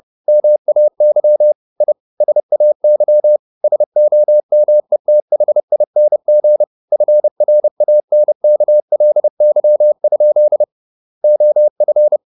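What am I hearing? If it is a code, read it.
Morse code, heard as a keyed tone, about 30 wpm, 600 Hz. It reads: MAY I SAY SOMETHING FRANKLY? OF
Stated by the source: Morse code at 30 wpm, the novel Never by Ken Follett in English, from CW med Gnister.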